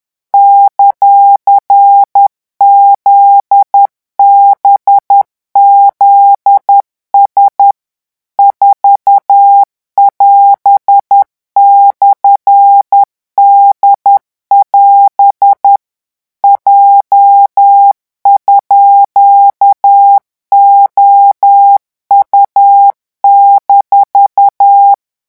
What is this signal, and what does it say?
Morse code, heard as a keyed tone, about 10 wpm, 800 Hz. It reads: ;ZBZS 4<AS>/D<AS> J_OU-
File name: morse mp3 / revision17-vitesse 10 mots